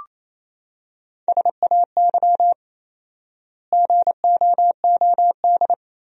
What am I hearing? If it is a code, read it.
Morse code, heard as a keyed tone, about 27 wpm, 700 Hz. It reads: SAY GOOD
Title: 4x Word-Spacing - Sentences from Top 300 Words - 28wpm